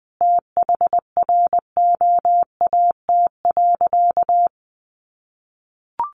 Morse code, heard as a keyed tone, about 20 wpm, 700 Hz. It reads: THROAT.
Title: A Sound Of Thunder 20wpm